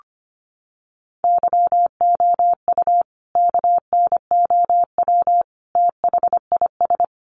Morse code, heard as a keyed tone, about 25 wpm, 700 Hz. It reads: YOU KNOW THIS
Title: Sentences from Top 800 Words 25wpm